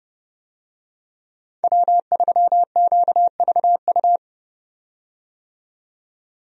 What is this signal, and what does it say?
Morse code, heard as a keyed tone, about 29 wpm, 700 Hz. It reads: W3QVU
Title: USA Call Signs - 5 Characters - 30wpm